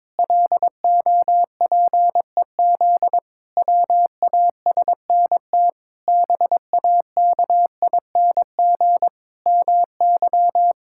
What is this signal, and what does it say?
Morse code, heard as a keyed tone, about 22 wpm, 700 Hz. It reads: LOPEZ WASNT BAKING MY
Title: Pangram Challenge v2 - 22wpm